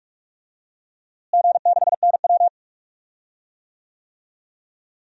Morse code, heard as a keyed tone, about 47 wpm, 700 Hz. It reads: G6NW
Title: DXCC-100 Call Signs - 4 Characters - Repeated 2x - 45wpm